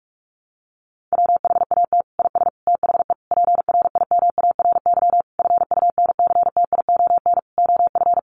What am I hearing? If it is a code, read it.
Morse code, heard as a keyed tone, about 44 wpm, 700 Hz. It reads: WHAT IS THE PRIMARY FUNCTION OF
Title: Sentences from Top 1100 Words 45wpm